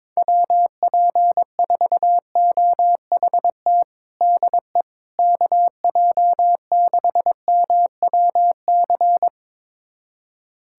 Morse code, heard as a keyed tone, about 22 wpm, 700 Hz. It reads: WP4OHT DE KJ6MWC
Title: USA Callsign DE Callsign 22wpm